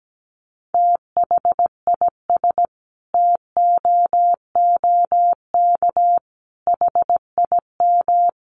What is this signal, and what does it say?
Morse code, heard as a keyed tone, about 17 wpm, 700 Hz. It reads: THIS TOOK HIM
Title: Sentences from Top 300 Words 17wpm